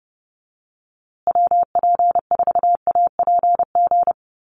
Morse code, heard as a keyed tone, about 30 wpm, 700 Hz. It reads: WP4APG
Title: USA Call Signs Repeated 3x 30wpm